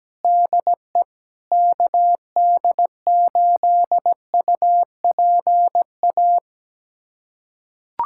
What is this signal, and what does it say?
Morse code, heard as a keyed tone, about 17 wpm, 700 Hz. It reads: DE KD8UPA
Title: USA Callsign DE Callsign - 6 Characters - 17wpm